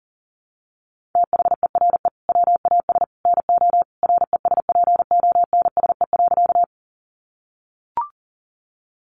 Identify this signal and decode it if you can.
Morse code, heard as a keyed tone, about 40 wpm, 700 Hz. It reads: THERE WAS NO RESPONSE.
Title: Sentences from Top 1400 Words 40wpm